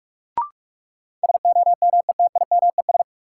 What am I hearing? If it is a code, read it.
Morse code, heard as a keyed tone, about 43 wpm, 700 Hz. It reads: SOMETIMES
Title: Sentences from Top 600 Words - New Word Drill - 45wpm